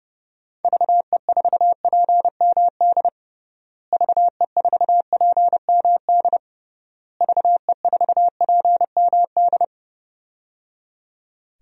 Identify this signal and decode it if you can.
Morse code, heard as a keyed tone, about 30 wpm, 700 Hz. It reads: VE4PMD VE4PMD VE4PMD
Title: DXCC-100 Call Signs - 6 Characters - Repeated 3x - 30wpm